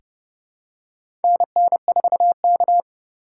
Morse code, heard as a keyed tone, about 30 wpm, 700 Hz. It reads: NN4K
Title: USA Call Signs - 2x Character Spacing - 30wpm